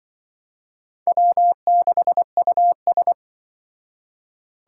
Morse code, heard as a keyed tone, about 23 wpm, 700 Hz. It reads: W6US
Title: Speed Racing - USA Call Signs 20wpm